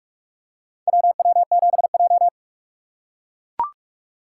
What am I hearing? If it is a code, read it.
Morse code, heard as a keyed tone, about 44 wpm, 690 Hz. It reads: WW7J
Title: USA Call Signs - 4 Characters - 3x Character Spacing - 45wpm